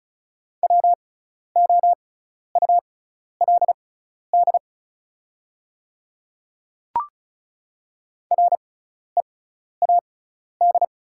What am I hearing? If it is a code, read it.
Morse code, heard as a keyed tone, about 36 wpm, spaced as for 12 wpm, 700 Hz. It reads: WOULD READ
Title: 6x Character Spacing - Top 200 Words - 35wpm